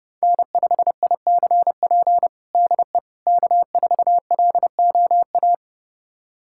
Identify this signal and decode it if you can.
Morse code, heard as a keyed tone, about 30 wpm, 700 Hz. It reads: N5ICP DE K4LOA
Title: USA Callsign DE Callsign - 5 Characters - 30wpm